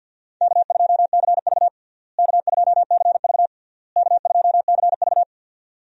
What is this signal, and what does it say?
Morse code, heard as a keyed tone, about 50 wpm, 700 Hz. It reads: K2CV K2CV K2CV
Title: USA Call Signs - 4 Characters - Repeated 3x - 50wpm